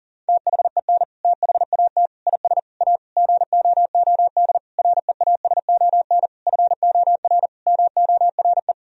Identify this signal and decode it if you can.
Morse code, heard as a keyed tone, about 40 wpm, 700 Hz. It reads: THEN THAT IS A GOOD REASON FOR MORE
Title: Sentences from Top 1000 Words - Review Part 4 - 40wpm